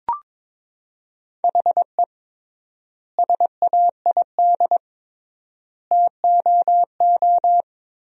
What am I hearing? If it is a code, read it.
Morse code, heard as a keyed tone, about 22 wpm, 700 Hz. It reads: HE SAID TOO
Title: 3x Word-Spacing - Sentences from Top 200 Words - 22wpm